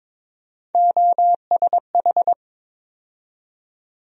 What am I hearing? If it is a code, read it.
Morse code, heard as a keyed tone, about 22 wpm, 700 Hz. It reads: OSH
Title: USA Call Sign Postfixes - 3 Letters 22wpm